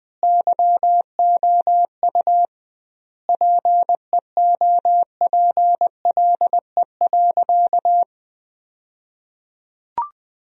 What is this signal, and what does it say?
Morse code, heard as a keyed tone, about 20 wpm, 700 Hz. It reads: YOU PEOPLE.
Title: 2x Word-Spacing - Sentences from Top 300 Words - 20wpm